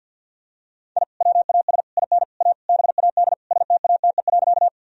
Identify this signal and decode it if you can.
Morse code, heard as a keyed tone, about 48 wpm, 700 Hz. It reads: I WAS IN A BAD STATE.
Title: Sentences from Top 1000 Words - Review Part 2 - 50wpm